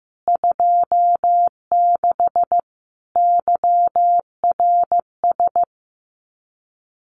Morse code, heard as a keyed tone, about 15 wpm, 700 Hz. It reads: 26 YRS
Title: Common QSO Phrases Encore 15wpm